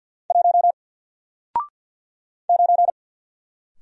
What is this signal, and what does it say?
Morse code, heard as a keyed tone, about 48 wpm, 700 Hz. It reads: 1 9
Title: Single Character (Letter:Number:Punctuation:Prosign) Alt 50wpm